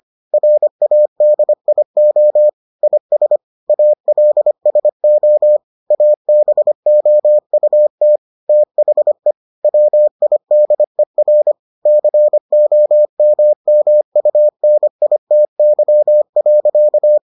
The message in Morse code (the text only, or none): RADIO IS ALSO ABOUT THE WIDER COMMUNITY.